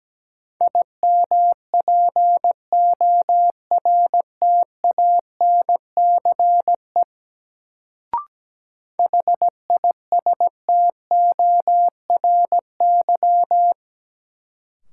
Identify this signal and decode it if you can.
Morse code, heard as a keyed tone, about 17 wpm, 700 Hz. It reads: IMPORTANCE HISTORY
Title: Sets of 3 Words - New Word Drill - 17wpm